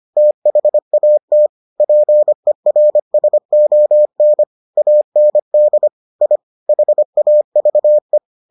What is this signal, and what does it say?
Morse code, heard as a keyed tone, about 25 wpm, 600 Hz. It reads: THAT PERSON AND I HAVE